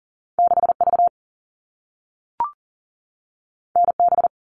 Morse code, heard as a keyed tone, about 41 wpm, 700 Hz. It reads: 6V NB
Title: All Cut Numbers - 2-Digit 40wpm